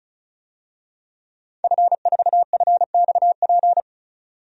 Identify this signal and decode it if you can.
Morse code, heard as a keyed tone, about 35 wpm, 700 Hz. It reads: F4FXP
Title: DXCC-100 Call Signs - 5 Characters - 35wpm